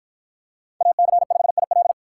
Morse code, heard as a keyed tone, about 50 wpm, 700 Hz. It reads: AC5IL